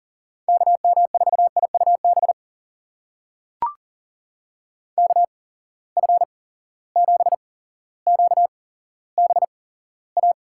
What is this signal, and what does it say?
Morse code, heard as a keyed tone, about 39 wpm, spaced as for 17 wpm, 700 Hz. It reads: KM4IUB KF7QBA
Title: USA Call Signs - 6 Characters - 8x Character Spacing - 40wpm